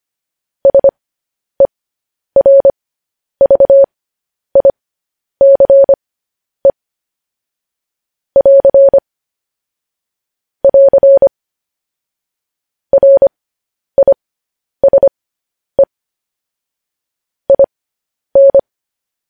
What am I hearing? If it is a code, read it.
Morse code, heard as a keyed tone, about 25 wpm, spaced as for 10 wpm, 550 Hz. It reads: SERVICE <AR> <AR> RISE IN
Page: news headlines in morse code at 10 wpm